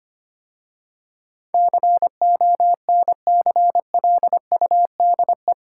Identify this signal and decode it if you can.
Morse code, heard as a keyed tone, about 25 wpm, 700 Hz. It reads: CONCLUDE
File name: 8 Letter Words - First 3 Letters Given - 25wpm